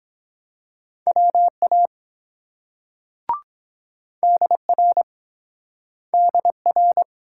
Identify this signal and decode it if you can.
Morse code, heard as a keyed tone, about 22 wpm, 700 Hz. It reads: WA DR DR